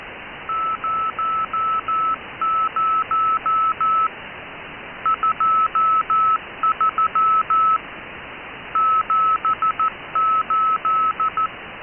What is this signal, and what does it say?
Morse code, heard as a keyed tone, about 14 wpm, 1280 Hz. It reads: 00 23 78